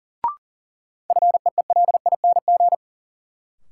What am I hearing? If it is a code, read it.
Morse code, heard as a keyed tone, about 40 wpm, 700 Hz. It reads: FEELING
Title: Sentences from Top 1000 Words - Review Part 3 - New Word Drill - 40wpm